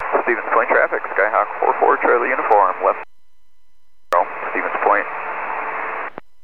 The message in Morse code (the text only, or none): none